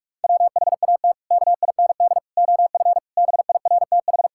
Morse code, heard as a keyed tone, about 46 wpm, 700 Hz. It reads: WHAT KIND OF BIRTH
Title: Sentences from Top 1000 Words - Review Part 5 - 45wpm